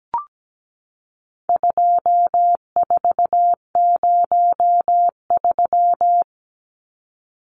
Morse code, heard as a keyed tone, about 17 wpm, 700 Hz. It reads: 2403
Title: Numbers - 4-Digit 17wpm